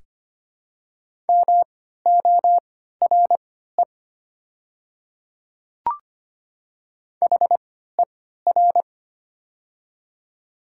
Morse code, heard as a keyed tone, about 25 wpm, spaced as for 14 wpm, 700 Hz. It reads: MORE HER
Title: 3x Character Spacing - Top 300 Words - 25wpm